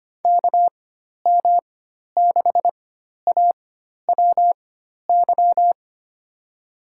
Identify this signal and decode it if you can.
Morse code, heard as a keyed tone, about 25 wpm, spaced as for 12 wpm, 700 Hz. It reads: KM6AWY
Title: USA Call Signs - 6 Characters - 4x Character Spacing - 25wpm